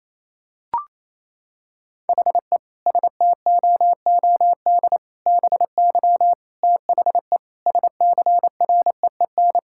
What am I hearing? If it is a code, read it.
Morse code, heard as a keyed tone, about 28 wpm, 700 Hz. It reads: HE STOOD BY THE SCREEN
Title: Sentences from Top 1000 Words - Review Part 6 - 28wpm